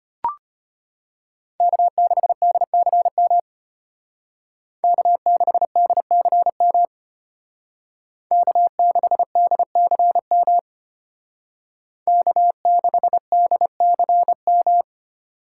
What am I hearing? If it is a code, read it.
Morse code, heard as a keyed tone, about 32 wpm, 700 Hz. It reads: K6DCM K6DCM K6DCM K6DCM